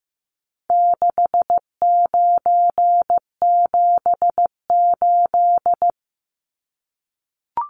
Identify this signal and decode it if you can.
Morse code, heard as a keyed tone, about 14 wpm, 700 Hz. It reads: 6978
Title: Numbers - 4-Digit 15wpm